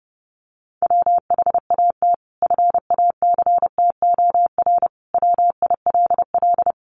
Morse code, heard as a keyed tone, about 30 wpm, 700 Hz. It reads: WHAT FACTOR WILL
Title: Sentences from Top 1300 Words 30wpm